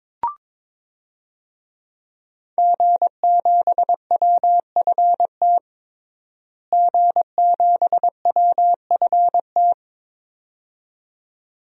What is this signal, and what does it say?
Morse code, heard as a keyed tone, about 22 wpm, 700 Hz. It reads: G7WFT G7WFT
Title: DXCC-100 Call Signs - 5 Characters - Repeated 2x - 22wpm